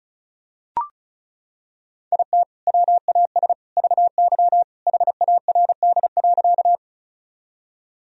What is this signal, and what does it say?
Morse code, heard as a keyed tone, about 36 wpm, 700 Hz. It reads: IT WAS VY HARD.
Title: Sentences from Top 300 Words and Abbreviations 35wpm